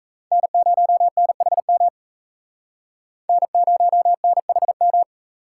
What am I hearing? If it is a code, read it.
Morse code, heard as a keyed tone, about 36 wpm, 700 Hz. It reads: N0NHM N0NHM